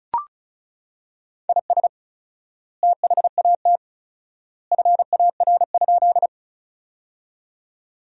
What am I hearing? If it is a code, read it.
Morse code, heard as a keyed tone, about 36 wpm, 700 Hz. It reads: IS THAT FAR?
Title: 4x Word-Spacing - Sentences from Top 300 Words - 35wpm